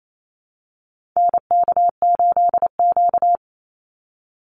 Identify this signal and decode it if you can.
Morse code, heard as a keyed tone, about 28 wpm, 700 Hz. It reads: NK8Q
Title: NA SOTA Activator Callsigns 28wpm